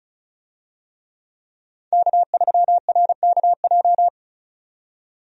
Morse code, heard as a keyed tone, about 34 wpm, 700 Hz. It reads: K3RKJ
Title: USA Call Signs - 5 Characters - 35wpm